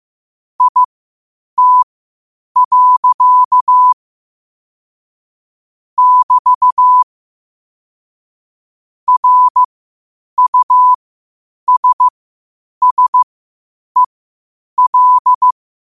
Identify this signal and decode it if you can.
Morse code, heard as a keyed tone, about 15 wpm, spaced as for 8 wpm, 1000 Hz. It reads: IT. <BT> RUSSEL
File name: 8wpm